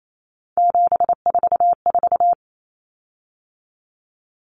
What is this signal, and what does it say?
Morse code, heard as a keyed tone, about 28 wpm, 700 Hz. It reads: 744
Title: Common Cut Numbers - 3-Digit 28wpm